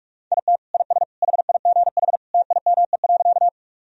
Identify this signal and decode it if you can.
Morse code, heard as a keyed tone, about 41 wpm, 700 Hz. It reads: IT IS HIGH TIME.
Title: Sentences from Top 1000 Words - Review Part 1 - 45wpm